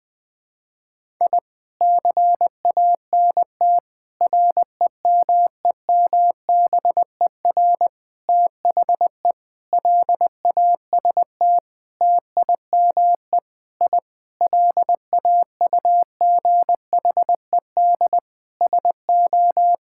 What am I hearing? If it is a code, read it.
Morse code, heard as a keyed tone, about 19 wpm, 700 Hz. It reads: I CANT REMEMBER THE LAST TIME I LAUGHED SO
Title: Sentences with 7-Word Phrases - Spoken Beginning - 20wpm